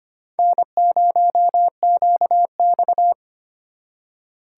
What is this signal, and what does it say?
Morse code, heard as a keyed tone, about 32 wpm, 700 Hz. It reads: N0QX